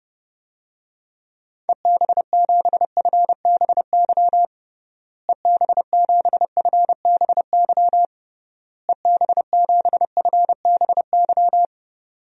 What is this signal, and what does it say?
Morse code, heard as a keyed tone, about 30 wpm, 700 Hz. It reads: EB7FBY EB7FBY EB7FBY
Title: DXCC-100 Call Signs - 6 Characters - Repeated 3x - 30wpm